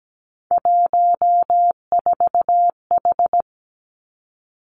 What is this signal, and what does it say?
Morse code, heard as a keyed tone, about 17 wpm, 700 Hz. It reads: 14H
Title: Three Challenging Letter-Number Combinations 17wpm